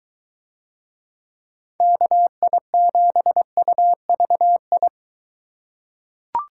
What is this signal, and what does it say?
Morse code, heard as a keyed tone, about 23 wpm, 700 Hz. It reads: KI7UVI